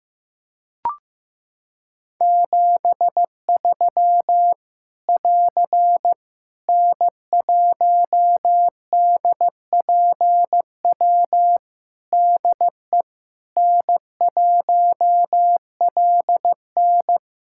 Morse code, heard as a keyed tone, about 15 wpm, 700 Hz. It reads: 73 <AR> N1DPW DE N1LN
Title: Practice QSO - Exchange 4 - Ending - 15wpm